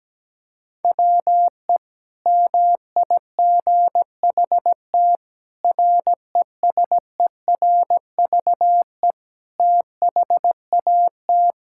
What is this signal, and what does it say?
Morse code, heard as a keyed tone, about 17 wpm, 700 Hz. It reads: WE MIGHT RESERVE THAT
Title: Sentences from Top 1000 Words - Review Part 6 - 17wpm